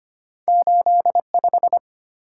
Morse code, heard as a keyed tone, about 25 wpm, 700 Hz. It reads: 85
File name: Two Letter-Number Combinations - 25wpm